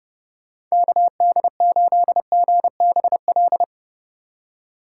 Morse code, heard as a keyed tone, about 40 wpm, spaced as for 35 wpm, 700 Hz. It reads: KD8GBL
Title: Speed Racing - USA 2x3 Call Signs 30wpm